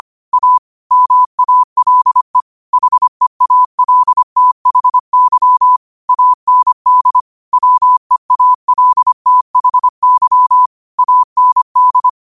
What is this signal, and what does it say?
Morse code, heard as a keyed tone, about 25 wpm, 1000 Hz. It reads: A MALE HEALTHY AND WEALTHY AND
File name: qotdfiles / 25wpm